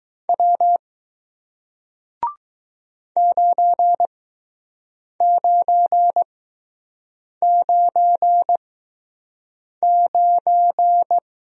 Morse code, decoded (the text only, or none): W 9999